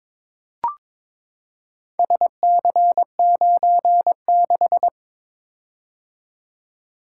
Morse code, heard as a keyed tone, about 22 wpm, 700 Hz. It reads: SC96
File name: practice / 22wpm / Four Letter-Number Combinations - 22wpm